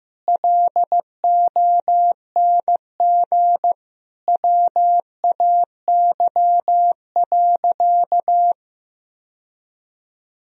Morse code, heard as a keyed tone, about 16 wpm, 700 Hz. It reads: LONG WAY.
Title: Sentences from Top 1000 Words - Review Part 1 - 15wpm